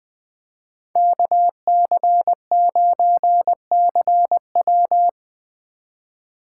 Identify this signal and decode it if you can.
Morse code, heard as a keyed tone, about 20 wpm, 700 Hz. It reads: KC9CW
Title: USA Call Signs - 2x Character Spacing - 20wpm